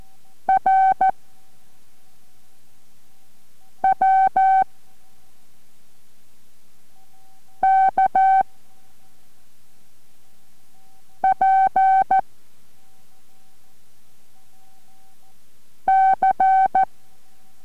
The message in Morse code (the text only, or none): R W K P C